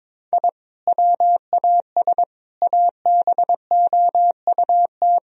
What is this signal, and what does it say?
Morse code, heard as a keyed tone, about 22 wpm, 700 Hz. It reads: I WAS ABOUT
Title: Sentences from Top 1000 Words - Review Part 1 - 22wpm